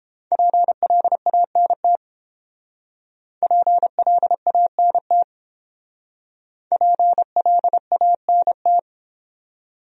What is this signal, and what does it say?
Morse code, heard as a keyed tone, about 29 wpm, 700 Hz. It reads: PLANT PLANT PLANT